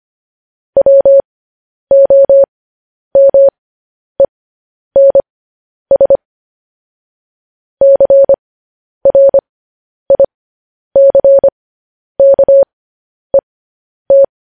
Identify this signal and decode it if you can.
Morse code, heard as a keyed tone, about 25 wpm, spaced as for 10 wpm, 550 Hz. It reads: WOMENS CRICKET
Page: news-headlines-in-morse-code-at-10-wpm